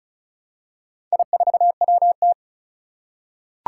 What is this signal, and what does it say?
Morse code, heard as a keyed tone, about 36 wpm, 700 Hz. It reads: I4WT
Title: Four Challenging Letter-Number Combinations 35wpm